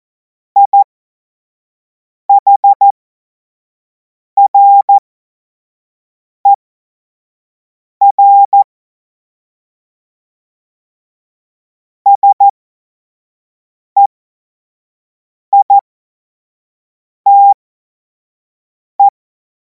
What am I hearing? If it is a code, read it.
Morse code, heard as a keyed tone, about 14 wpm, spaced as for 5 wpm, 800 Hz. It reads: IHRER SEITE
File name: Morsekurs / Witze / Witz 10. März 2026 25 F70